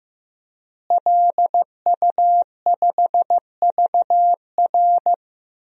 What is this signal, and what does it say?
Morse code, heard as a keyed tone, about 16 wpm, 700 Hz. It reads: LU5VR
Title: DXCC-100 Call Signs - 5 Characters - Repeated 3x - 15wpm